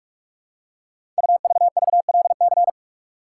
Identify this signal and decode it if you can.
Morse code, heard as a keyed tone, about 44 wpm, 700 Hz. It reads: UVVLC
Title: Code Groups - 5 Letters - 45wpm